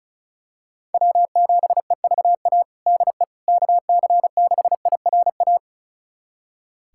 Morse code, heard as a keyed tone, about 36 wpm, 700 Hz. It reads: W7EVA DE KC6IRA
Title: USA Callsign DE Callsign 35wpm